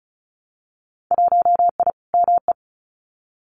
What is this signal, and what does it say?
Morse code, heard as a keyed tone, about 34 wpm, 700 Hz. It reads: 1I ME
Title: Winter Field Day Exchanges 35wpm